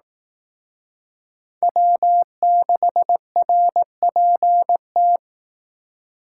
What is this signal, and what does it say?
Morse code, heard as a keyed tone, about 18 wpm, 700 Hz. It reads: W6RPT
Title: Speed Racing - USA 1x3 Call Signs 15wpm